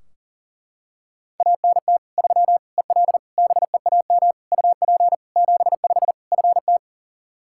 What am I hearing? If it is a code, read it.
Morse code, heard as a keyed tone, about 37 wpm, 700 Hz. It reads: ANT 3 EL BEAM UP 75 FT
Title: Practice QSO - Exchange 2 - Info - 40wpm